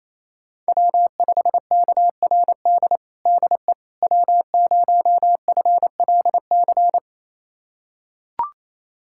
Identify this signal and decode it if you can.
Morse code, heard as a keyed tone, about 28 wpm, 700 Hz. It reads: W5KRD DE W0FLC
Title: USA Callsign DE Callsign - 5 Characters - 28wpm